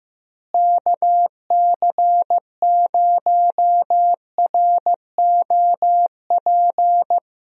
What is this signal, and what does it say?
Morse code, heard as a keyed tone, about 15 wpm, 700 Hz. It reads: KC0ROP
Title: USA Call Signs - 6 Characters - Repeated 2x - 15wpm